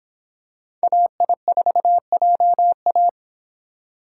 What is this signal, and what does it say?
Morse code, heard as a keyed tone, about 24 wpm, 700 Hz. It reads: AI4JA